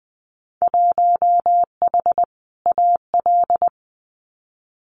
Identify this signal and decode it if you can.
Morse code, heard as a keyed tone, about 20 wpm, 700 Hz. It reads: 1H AL
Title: Winter Field Day Exchanges 20wpm